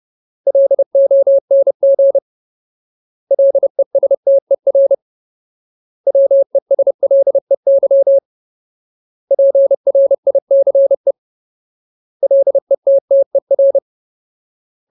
Morse code, heard as a keyed tone, about 30 wpm, 550 Hz. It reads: LONG LESTER WESLEY PRICE LETTER
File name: Ultimate Word List / 30WPM Ultimate